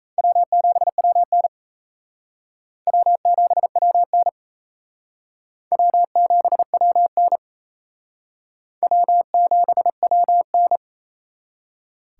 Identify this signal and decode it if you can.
Morse code, heard as a keyed tone, about 34 wpm, 700 Hz. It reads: W7WN W7WN W7WN W7WN
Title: Speed Racing - USA 1x2 Call Signs 28wpm